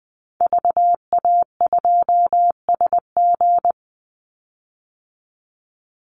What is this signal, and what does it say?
Morse code, heard as a keyed tone, about 20 wpm, 700 Hz. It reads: VA2SG
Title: Call Signs Repeated 2x - 20wpm